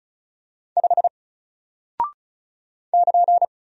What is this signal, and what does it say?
Morse code, heard as a keyed tone, about 36 wpm, 700 Hz. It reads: 5 <KN>
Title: Single Character (Letter:Number:Punctuation:Prosign) Alt 35wpm